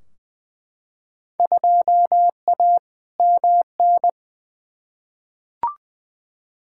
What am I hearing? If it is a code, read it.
Morse code, heard as a keyed tone, about 20 wpm, 700 Hz. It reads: 2A MN
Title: Field Day Exchanges - 20wpm